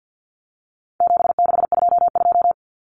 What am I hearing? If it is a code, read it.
Morse code, heard as a keyed tone, about 50 wpm, 700 Hz. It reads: 7622
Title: Common Cut Numbers - 4-Digit 50wpm